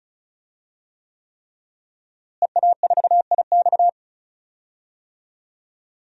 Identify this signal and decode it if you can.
Morse code, heard as a keyed tone, about 34 wpm, 700 Hz. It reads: EA4IX
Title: DXCC-100 Call Signs - 5 Characters - Repeated 3x - 35wpm